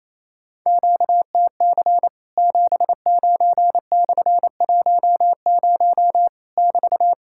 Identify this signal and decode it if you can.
Morse code, heard as a keyed tone, about 28 wpm, 700 Hz. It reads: QTC 79/10 <BT>
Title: Worked All Europe DX Contest - QTC Exchange - 28wpm